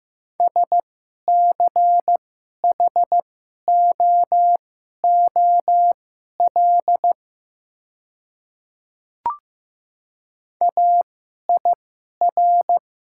Morse code, heard as a keyed tone, about 15 wpm, spaced as for 11 wpm, 700 Hz. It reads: SCHOOL AIR
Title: 2x Character Spacing - Top 300 Words - 15wpm